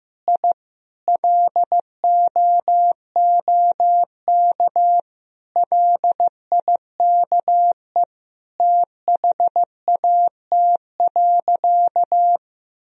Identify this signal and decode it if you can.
Morse code, heard as a keyed tone, about 15 wpm, 700 Hz. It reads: I LOOK LIKE THAT.
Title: Sentences from Top 1000 Words - Review Part 1 - 15wpm